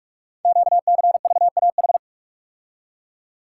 Morse code, heard as a keyed tone, about 44 wpm, 700 Hz. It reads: QCVAH
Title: Code Groups - 5 Letters - 45wpm